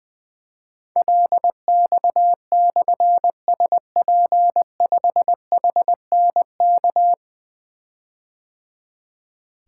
Morse code, heard as a keyed tone, about 20 wpm, 700 Hz. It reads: LX/SP5HNK
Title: Difficult Call Signs 20wpm